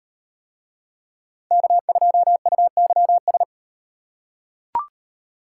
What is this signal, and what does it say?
Morse code, heard as a keyed tone, about 38 wpm, 700 Hz. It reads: K2UYS